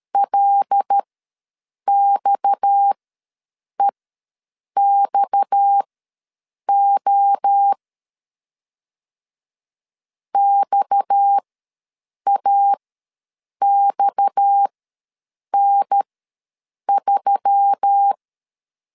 Morse code, heard as a keyed tone, about 13 wpm, spaced as for 7 wpm, 800 Hz. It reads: LXEXO XAXN3
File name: LEZIONI / 6 PKX0 / 6 PKX ws 12wpm 3106